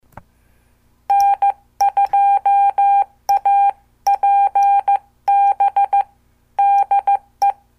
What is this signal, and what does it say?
Morse code, heard as a keyed tone, about 14 wpm, 800 Hz. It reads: N2APB DE
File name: morse / qso